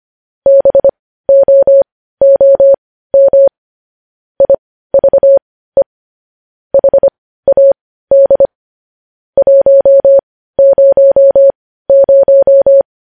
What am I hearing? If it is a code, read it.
Morse code, heard as a keyed tone, about 25 wpm, spaced as for 15 wpm, 550 Hz. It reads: BOOM IVE HAD 100
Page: News Headlines in Morse Code at 15 WPM